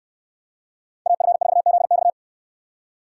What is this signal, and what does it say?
Morse code, heard as a keyed tone, about 68 wpm, 700 Hz. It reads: AF4ZX